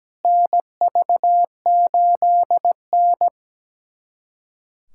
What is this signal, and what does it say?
Morse code, heard as a keyed tone, about 17 wpm, 700 Hz. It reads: NV8N
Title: USA Callsign DE Callsign - 4 Characters - 17wpm